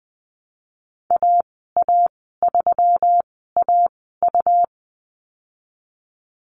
Morse code, heard as a keyed tone, about 20 wpm, spaced as for 14 wpm, 700 Hz. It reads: AA3AU
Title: 128 Lesson 21g - Number 6 - Call Signs 20wpm